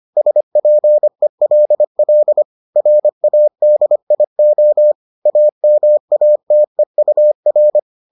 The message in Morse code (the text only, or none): SPELL RADIO AMATEUR